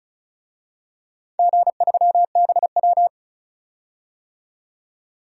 Morse code, heard as a keyed tone, about 36 wpm, 700 Hz. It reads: G3BW
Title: DXCC-100 Call Signs - 4 Characters - 35wpm